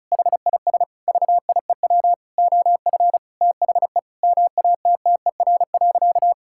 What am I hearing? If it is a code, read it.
Morse code, heard as a keyed tone, about 36 wpm, 700 Hz. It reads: HIS VIEW OF THE MATTER.
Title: Sentences from Top 1000 Words - Review Part 6 - 35wpm